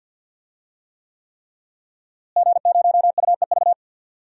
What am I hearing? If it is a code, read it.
Morse code, heard as a keyed tone, about 51 wpm, 700 Hz. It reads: G0UEV
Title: DXCC-100 Call Signs - 5 Characters - Repeated 3x - 50wpm